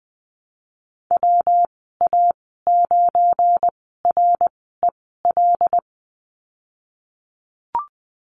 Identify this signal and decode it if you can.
Morse code, heard as a keyed tone, about 20 wpm, spaced as for 15 wpm, 700 Hz. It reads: WA9REL
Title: 186 Lesson 30g - Letter b - Call Signs 20wpm